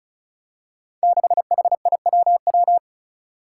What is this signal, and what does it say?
Morse code, heard as a keyed tone, about 36 wpm, 700 Hz. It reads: 6HIWW
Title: Five Challenging Letter-Number Combinations 35wpm